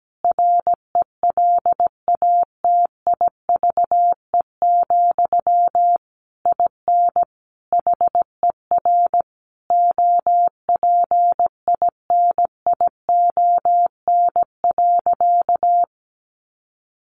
Morse code, heard as a keyed tone, about 17 wpm, 700 Hz. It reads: RELATIVE, IN HER OPINION.
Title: Sentences from Top 1400 Words 17wpm